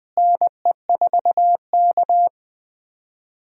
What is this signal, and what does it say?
Morse code, heard as a keyed tone, about 20 wpm, 700 Hz. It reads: NE4K
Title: USA Call Signs - 4 Characters - Repeated 3x - 20wpm